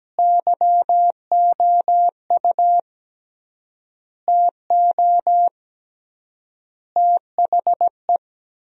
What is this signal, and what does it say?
Morse code, heard as a keyed tone, about 17 wpm, 700 Hz. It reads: YOU TO THE
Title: 3x Word-Spacing - Sentences from Top 400 Words - 17wpm